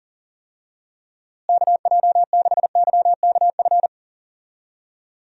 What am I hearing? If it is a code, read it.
Morse code, heard as a keyed tone, about 40 wpm, 700 Hz. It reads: KJ6YKF
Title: USA Call Signs - 6 Characters - Repeated 2x - 40wpm